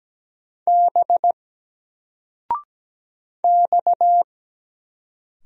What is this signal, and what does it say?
Morse code, heard as a keyed tone, about 17 wpm, 700 Hz. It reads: B X
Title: Single Letter-Number 17wpm